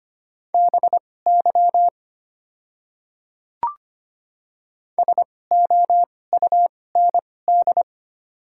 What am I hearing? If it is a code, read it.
Morse code, heard as a keyed tone, about 25 wpm, spaced as for 18 wpm, 700 Hz. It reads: BY SOUND
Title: 2x Character Spacing - Top 200 Words - 25wpm